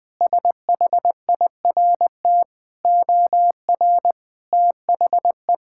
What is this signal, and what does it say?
Morse code, heard as a keyed tone, about 21 wpm, 700 Hz. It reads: SHIRT OR THE
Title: Sentences with 2-Word Phrases - 20wpm